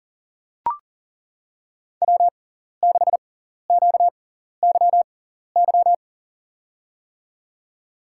Morse code, heard as a keyed tone, about 40 wpm, spaced as for 14 wpm, 700 Hz. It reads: W6QYY
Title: USA Call Signs - 5 Characters - 6x Character Spacing - 40wpm